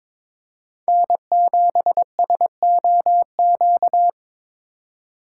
Morse code, heard as a keyed tone, about 22 wpm, 700 Hz. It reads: N7SOQ
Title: USA Call Signs - 5 Characters - 2x Character Spacing - 22wpm